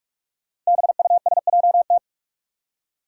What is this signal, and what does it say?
Morse code, heard as a keyed tone, about 46 wpm, 700 Hz. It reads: BUSJT
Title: Five Challenging Letter-Number Combinations 45wpm